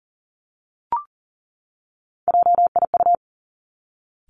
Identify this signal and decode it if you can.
Morse code, heard as a keyed tone, about 38 wpm, 700 Hz. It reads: JIU
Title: Three Challenging Letter-Number Combinations 40wpm